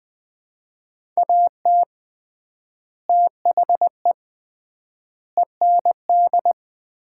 Morse code, heard as a keyed tone, about 20 wpm, 700 Hz. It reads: AT THE END
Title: 3x Word-Spacing - Sets of 4 Words - 20wpm